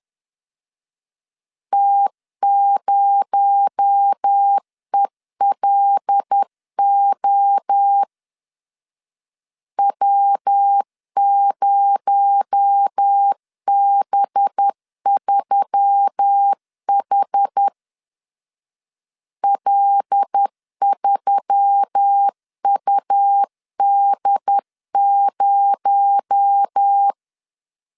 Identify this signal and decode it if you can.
Morse code, heard as a keyed tone, about 10 wpm, 800 Hz. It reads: T0ELO W0B3H L3UD0
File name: LEZIONI / 6 PKX0 / 6 PKX0 10wpm 2707